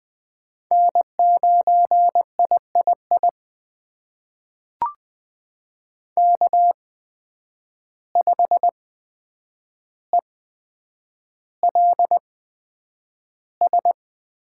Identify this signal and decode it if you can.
Morse code, heard as a keyed tone, about 20 wpm, spaced as for 9 wpm, 700 Hz. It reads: N9III K5ELS
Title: USA Call Signs - 5 Characters - 8x Character Spacing - 20wpm